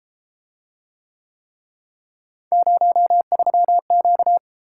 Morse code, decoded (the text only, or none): 03Q